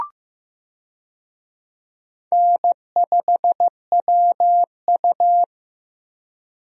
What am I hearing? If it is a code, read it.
Morse code, heard as a keyed tone, about 15 wpm, 700 Hz. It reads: N5WU
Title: USA Call Signs - 4 Characters - 15wpm